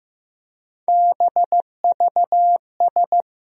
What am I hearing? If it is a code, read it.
Morse code, heard as a keyed tone, about 15 wpm, 700 Hz. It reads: BVS
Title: USA Call Sign Postfixes - 3 Letters 15wpm